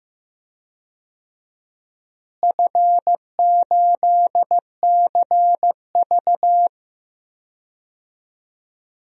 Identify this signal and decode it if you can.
Morse code, heard as a keyed tone, about 15 wpm, 700 Hz. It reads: F8CV
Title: DXCC-100 Call Signs - 4 Characters - Repeated 3x - 15wpm